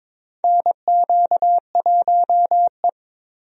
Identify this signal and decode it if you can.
Morse code, heard as a keyed tone, about 22 wpm, 700 Hz. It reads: NQ1E
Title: USA Call Signs - 4 Characters - 22wpm